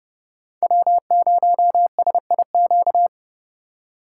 Morse code, heard as a keyed tone, about 30 wpm, 700 Hz. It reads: W0SIQ